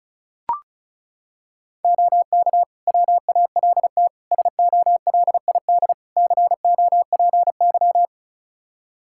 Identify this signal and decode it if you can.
Morse code, heard as a keyed tone, about 34 wpm, 700 Hz. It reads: OK WALT SOLID COPY
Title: Practice QSO - Exchange 3 - More Info - 35wpm